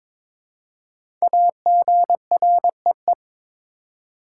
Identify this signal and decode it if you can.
Morse code, heard as a keyed tone, about 22 wpm, 700 Hz. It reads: AGREE
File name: 5 Letter Words - First 3 Letters Given - 22wpm